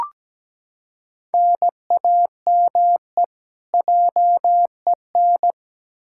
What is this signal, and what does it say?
Morse code, heard as a keyed tone, about 17 wpm, 700 Hz. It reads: NAME JEN
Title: K1USN SST QSO Practice 17wpm